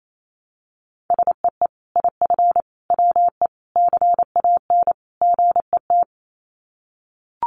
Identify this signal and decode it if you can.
Morse code, heard as a keyed tone, about 28 wpm, 700 Hz. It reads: SEE IF WE CAN GET
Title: Sets of 5 Words Encore2 28wpm